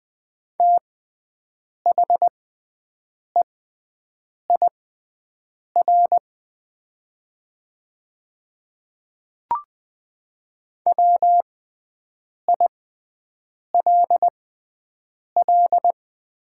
Morse code, heard as a keyed tone, about 20 wpm, spaced as for 7 wpm, 700 Hz. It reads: THEIR WILL
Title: 6x Character Spacing - Top 100 Words - 20wpm